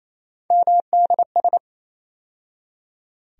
Morse code, heard as a keyed tone, about 28 wpm, 700 Hz. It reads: MDS
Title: USA Call Sign Postfixes - 3 Letters 28wpm